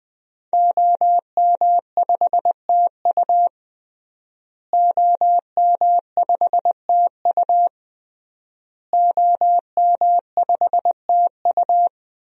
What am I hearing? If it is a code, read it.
Morse code, heard as a keyed tone, about 20 wpm, 700 Hz. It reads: OM5TU OM5TU OM5TU
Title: DXCC-100 Call Signs - 5 Characters - Repeated 3x - 20wpm